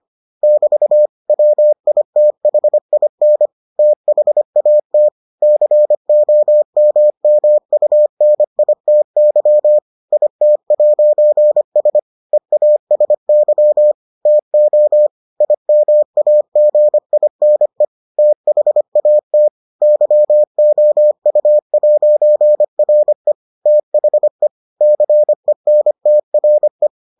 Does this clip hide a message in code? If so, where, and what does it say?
Yes: Morse code from 0.4 to 26.9 s, heard as a keyed tone, about 25 wpm, 600 Hz. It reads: <BT> WITHIN THAT COMMUNITY IT'S EASY TO IMAGINE THAT YOU'RE THE CENTRE